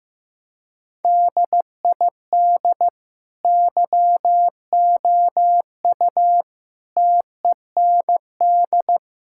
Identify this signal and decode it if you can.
Morse code, heard as a keyed tone, about 15 wpm, 700 Hz. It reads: DID YOU TEND
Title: Sentences from Top 1000 Words - Review Part 5 - 15wpm